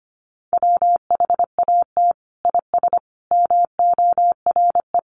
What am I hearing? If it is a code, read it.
Morse code, heard as a keyed tone, about 25 wpm, 700 Hz. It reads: WHAT IS MORE
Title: Sentences from Top 1400 Words 25wpm